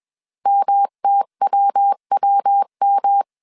Morse code, heard as a keyed tone, about 21 wpm, 800 Hz. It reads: MTWWM
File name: CW / DET / 1 3 1500C S B 20wpm 1754